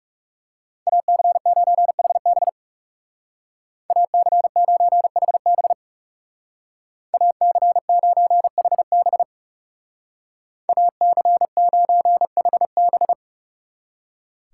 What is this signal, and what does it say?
Morse code, heard as a keyed tone, about 37 wpm, 700 Hz. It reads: AC9HB AC9HB AC9HB AC9HB
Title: Speed Racing - USA 2x2 Call Signs 30wpm